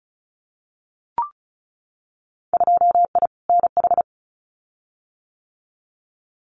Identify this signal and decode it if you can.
Morse code, heard as a keyed tone, about 36 wpm, 700 Hz. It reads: 2I NH
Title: Winter Field Day Exchanges Encore 35wpm